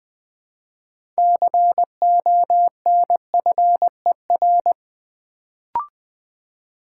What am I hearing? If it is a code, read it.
Morse code, heard as a keyed tone, about 20 wpm, 700 Hz. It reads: CONFER
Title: Build A Word - 3 Letters At A Time - 6-Letter Words - 20wpm